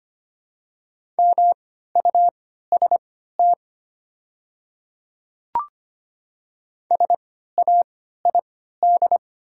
Morse code, heard as a keyed tone, about 25 wpm, spaced as for 14 wpm, 700 Hz. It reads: MUST SAID
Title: 3x Character Spacing - Top 300 Words - 25wpm